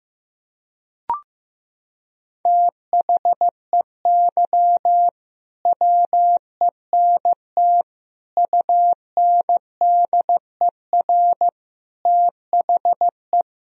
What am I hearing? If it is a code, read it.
Morse code, heard as a keyed tone, about 15 wpm, 700 Hz. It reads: THEY WENT UNDER THE
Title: Sentences from Top 1000 Words - Review Part 1 - 15wpm